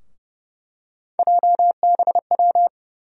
Morse code, heard as a keyed tone, about 30 wpm, 700 Hz. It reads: JBW
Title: USA Call Sign Postfixes - 3 Letters 30wpm